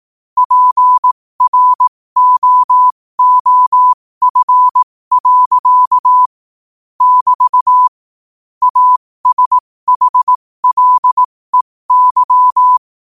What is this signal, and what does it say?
Morse code, heard as a keyed tone, about 18 wpm, spaced as for 15 wpm, 1000 Hz. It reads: PROOF. <BT> ASHLEY
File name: 15wpm